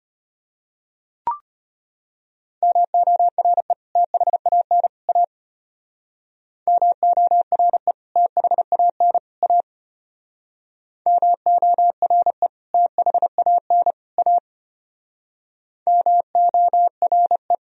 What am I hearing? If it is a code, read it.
Morse code, heard as a keyed tone, about 32 wpm, 700 Hz. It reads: MORE THAN A MORE THAN A MORE THAN A MORE